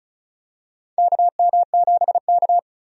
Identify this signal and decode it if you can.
Morse code, heard as a keyed tone, about 36 wpm, 700 Hz. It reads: KM7K
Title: USA Call Signs - 4 Characters - Repeated 3x - 35wpm